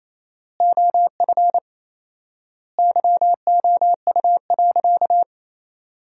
Morse code, heard as a keyed tone, about 28 wpm, 700 Hz. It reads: OF YOU.
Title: 4x Word-Spacing - Sentences from Top 200 Words - 28wpm